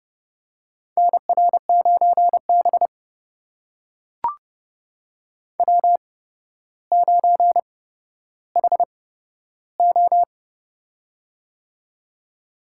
Morse code, heard as a keyed tone, about 30 wpm, spaced as for 13 wpm, 700 Hz. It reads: NR9B W9HO
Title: USA Call Signs - 4 Characters - 8x Character Spacing - 30wpm